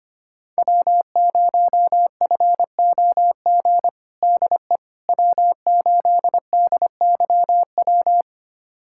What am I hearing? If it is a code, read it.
Morse code, heard as a keyed tone, about 25 wpm, 700 Hz. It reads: W0FOG DE W8DYW
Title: USA Callsign DE Callsign - 5 Characters - 25wpm